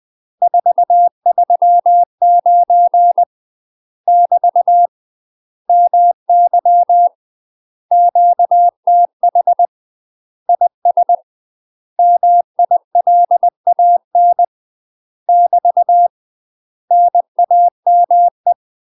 Morse code, heard as a keyed tone, about 20 wpm, 700 Hz. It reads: 439 <BT> MY QTH IS MILAN <BT> NAME
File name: UK MASTER QSOs TRAINING / 20WPM / UK MASTER QSOs 0048